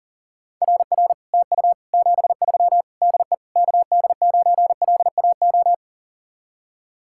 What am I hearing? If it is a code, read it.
Morse code, heard as a keyed tone, about 49 wpm, 700 Hz. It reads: RR TU 73 DE KD9LAO
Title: Speed-Racing Common QSO Phrases Encore - 40wpm